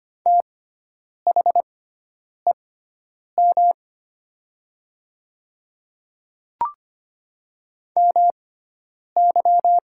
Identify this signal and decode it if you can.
Morse code, heard as a keyed tone, about 25 wpm, spaced as for 9 wpm, 700 Hz. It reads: THEM MY
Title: 6x Character Spacing - Top 100 Words - 25wpm